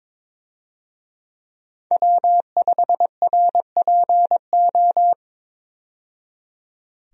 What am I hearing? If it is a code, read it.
Morse code, heard as a keyed tone, about 22 wpm, 700 Hz. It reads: W5RPO
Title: USA Call Signs - 5 Characters - 22wpm